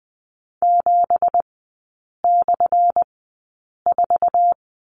Morse code, heard as a keyed tone, about 20 wpm, spaced as for 8 wpm, 700 Hz. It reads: 7/4
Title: 212 Lesson 34e - Number 8 - Three Characters 20wpm